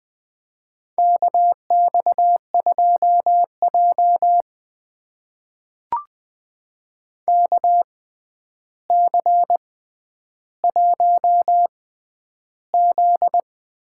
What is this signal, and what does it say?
Morse code, heard as a keyed tone, about 20 wpm, spaced as for 10 wpm, 700 Hz. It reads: KX2J KC1Z
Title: USA Call Signs - 6x Character Spacing - 20wpm